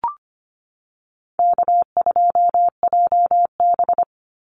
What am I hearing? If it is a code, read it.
Morse code, heard as a keyed tone, about 25 wpm, 700 Hz. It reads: K2JB
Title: NA SOTA Activator Callsigns 25wpm